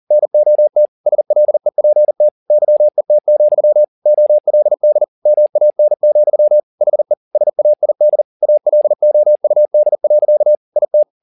Morse code, heard as a keyed tone, about 40 wpm, 600 Hz. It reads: NOT SLEPT YET, OLD MAN, HE SAID ALOUD. IT
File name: the old man and the sea 2 40wpm